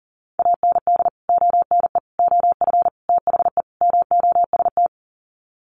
Morse code, heard as a keyed tone, about 40 wpm, 700 Hz. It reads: AND ONE OF THE MOST
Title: Sets of 5 Words Encore 40wpm